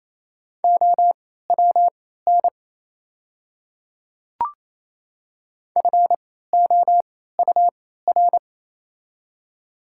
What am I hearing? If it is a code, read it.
Morse code, heard as a keyed tone, about 28 wpm, spaced as for 16 wpm, 700 Hz. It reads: OWN FOUR
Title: 3x Character Spacing - Top 500 Words - 28wpm